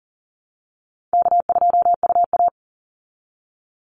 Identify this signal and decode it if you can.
Morse code, heard as a keyed tone, about 40 wpm, 700 Hz. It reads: K2UA
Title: Easy Call Signs Repeated 2x - 40wpm